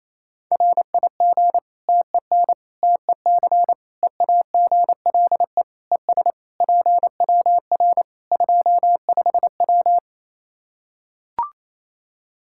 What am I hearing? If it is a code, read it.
Morse code, heard as a keyed tone, about 28 wpm, 700 Hz. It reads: RIG TEN TEC EAGLE ES PWR 25W
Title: K1USN SST QSO Practice 28wpm